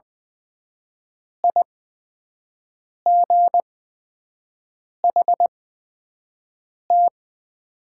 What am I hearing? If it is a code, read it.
Morse code, heard as a keyed tone, about 20 wpm, spaced as for 5 wpm, 700 Hz. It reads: IGHT
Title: 8x Character Spacing - Top 500 Words - 20wpm